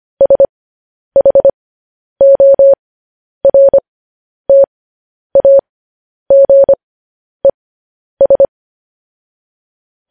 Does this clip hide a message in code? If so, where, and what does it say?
Yes: Morse code from 0.2 to 8.4 s, heard as a keyed tone, about 25 wpm, spaced as for 10 wpm, 550 Hz. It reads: SHORTAGES